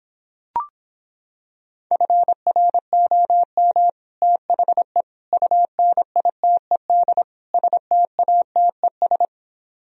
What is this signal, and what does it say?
Morse code, heard as a keyed tone, about 26 wpm, 700 Hz. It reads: FROM THE UNITED STATES